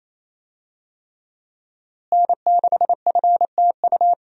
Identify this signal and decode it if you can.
Morse code, heard as a keyed tone, about 28 wpm, 700 Hz. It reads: N6FTU
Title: USA Call Signs - 5 Characters - Repeated 3x - 28wpm